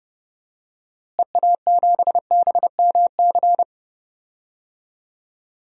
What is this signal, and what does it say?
Morse code, heard as a keyed tone, about 30 wpm, 700 Hz. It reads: EA7BMC
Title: DXCC-100 Call Signs - 6 Characters - Repeated 2x - 30wpm